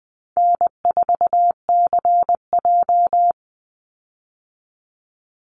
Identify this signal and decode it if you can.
Morse code, heard as a keyed tone, about 20 wpm, 700 Hz. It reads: N4CJ
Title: Easy Call Signs Repeated 3x - 20wpm